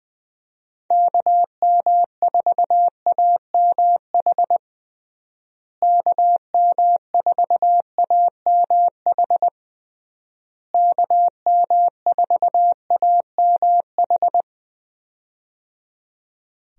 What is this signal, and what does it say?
Morse code, heard as a keyed tone, about 20 wpm, 700 Hz. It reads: KM4AMH KM4AMH KM4AMH
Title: USA Call Signs - 6 Characters - Repeated 3x - 20wpm